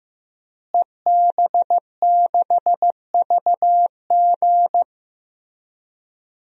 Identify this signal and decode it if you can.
Morse code, heard as a keyed tone, about 15 wpm, 700 Hz. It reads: EB6VG
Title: DXCC-100 Call Signs - 5 Characters - Repeated 2x - 15wpm